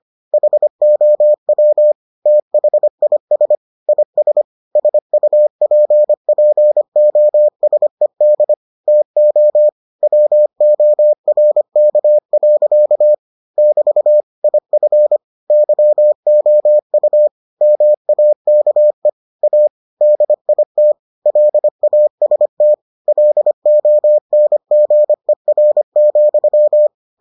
Morse code, heard as a keyed tone, about 25 wpm, 600 Hz. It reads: HOW THIS IS SUPPOSED TO WORK. <BT> IF YOU MAKE A DIT LAST LONGER,